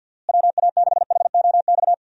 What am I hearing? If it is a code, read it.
Morse code, heard as a keyed tone, about 51 wpm, 700 Hz. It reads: WA6HOX
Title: USA Call Signs - 6 Characters - Repeated 2x - 50wpm